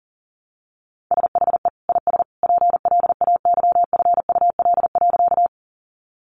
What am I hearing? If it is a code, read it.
Morse code, heard as a keyed tone, about 40 wpm, 700 Hz. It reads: SHE IS PLAYFUL.
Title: Easy Phrases 40wpm